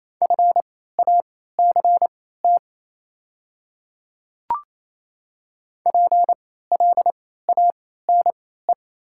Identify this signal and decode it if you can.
Morse code, heard as a keyed tone, about 28 wpm, spaced as for 16 wpm, 690 Hz. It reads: FACT PLANE
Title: 3x Character Spacing - Top 500 Words - 28wpm